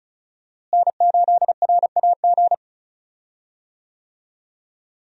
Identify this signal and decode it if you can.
Morse code, heard as a keyed tone, about 34 wpm, 700 Hz. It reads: N8RAG
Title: USA Call Signs - 5 Characters - Repeated 2x - 35wpm